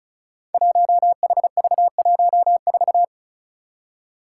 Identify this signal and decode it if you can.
Morse code, heard as a keyed tone, about 36 wpm, 700 Hz. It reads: 1HV14
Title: Five Challenging Letter-Number Combinations 35wpm